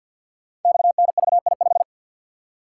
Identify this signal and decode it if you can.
Morse code, heard as a keyed tone, about 50 wpm, 700 Hz. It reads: XNVI5
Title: Code Groups - 5 Letter:Numbers - 50wpm